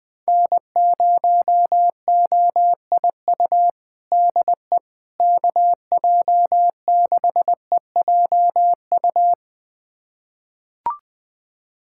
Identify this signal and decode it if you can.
Morse code, heard as a keyed tone, about 20 wpm, 700 Hz. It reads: N0OIU DE KJ6EJU
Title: USA Callsign DE Callsign 20wpm